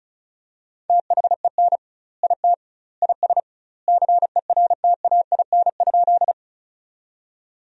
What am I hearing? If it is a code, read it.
Morse code, heard as a keyed tone, about 36 wpm, 700 Hz. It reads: THEN IT IS CERTAIN?
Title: 2x Word-Spacing - Sentences from Top 500 Words - 35wpm